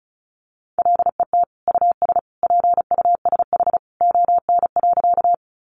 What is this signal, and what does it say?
Morse code, heard as a keyed tone, about 34 wpm, 700 Hz. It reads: LET US PUSH ON.
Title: Sentences from Top 1300 Words 35wpm